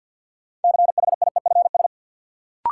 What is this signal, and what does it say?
Morse code, heard as a keyed tone, about 52 wpm, 700 Hz. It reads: CHIEFS
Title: Sets of 5 Words Encore - New Word Drill - 50wpm